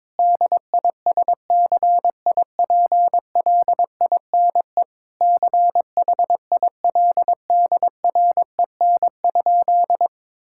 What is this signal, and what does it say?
Morse code, heard as a keyed tone, about 21 wpm, 700 Hz. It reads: DISCIPLINE CHILDREN?
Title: Sentences from Top 1000 Words - Review Part 6 - 22wpm